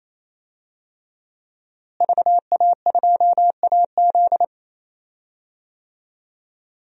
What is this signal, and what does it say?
Morse code, heard as a keyed tone, about 28 wpm, 700 Hz. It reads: VA2AZ
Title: DXCC-100 Call Signs - 5 Characters - Repeated 3x - 28wpm